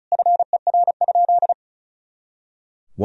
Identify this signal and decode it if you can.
Morse code, heard as a keyed tone, about 34 wpm, 700 Hz. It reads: FER?
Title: Sentences from Top 200 Words and Abbreviations 35wpm